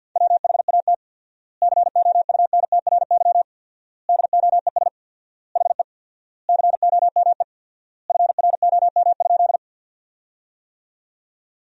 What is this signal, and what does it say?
Morse code, heard as a keyed tone, about 48 wpm, 700 Hz. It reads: WHAT COUNTRY DOES HE COME FROM?
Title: 4x Word-Spacing - Sentences from Top 300 Words - 50wpm